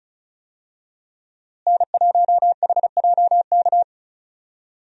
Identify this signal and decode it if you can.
Morse code, heard as a keyed tone, about 36 wpm, 700 Hz. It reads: N1HJK
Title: USA Call Signs - 5 Characters - Repeated 2x - 35wpm